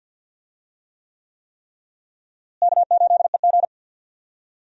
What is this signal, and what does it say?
Morse code, heard as a keyed tone, about 51 wpm, 700 Hz. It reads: K8EG
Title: USA Call Signs - 4 Characters - Repeated 3x - 50wpm